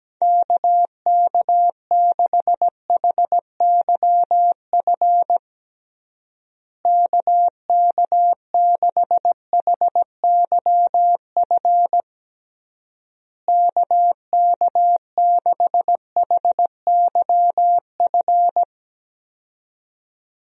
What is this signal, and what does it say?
Morse code, heard as a keyed tone, about 17 wpm, 700 Hz. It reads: KK6HYF KK6HYF KK6HYF
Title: USA Call Signs - 6 Characters - Repeated 3x - 17wpm